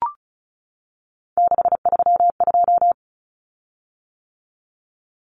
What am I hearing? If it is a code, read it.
Morse code, heard as a keyed tone, about 34 wpm, 700 Hz. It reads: 632
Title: Common Cut Numbers - 3-Digit 35wpm